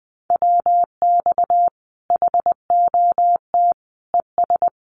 Morse code, heard as a keyed tone, about 20 wpm, 700 Hz. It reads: WX HOT ES
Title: Common QSO Phrases 20wpm